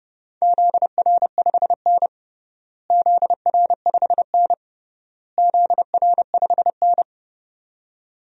Morse code, heard as a keyed tone, about 30 wpm, 700 Hz. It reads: ZR5N ZR5N ZR5N
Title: DXCC-100 Call Signs - 4 Characters - Repeated 3x - 30wpm